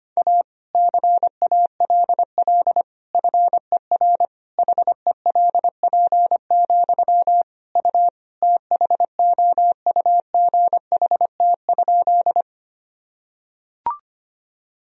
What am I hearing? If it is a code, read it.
Morse code, heard as a keyed tone, about 25 wpm, 700 Hz. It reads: A CALL FER HELP, U THOUGHT?
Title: Sentences from Top 300 Words and Abbreviations 25wpm